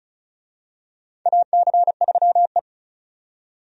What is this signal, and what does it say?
Morse code, heard as a keyed tone, about 34 wpm, 700 Hz. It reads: AC3E
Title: USA Call Signs - 4 Characters - 35wpm